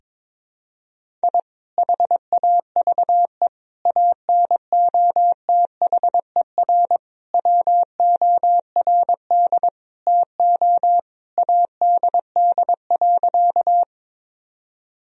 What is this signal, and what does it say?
Morse code, heard as a keyed tone, about 21 wpm, 700 Hz. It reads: I HAVE ANOTHER WORD TO ADD.
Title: Sentences from Top 1000 Words - Review Part 1 - 22wpm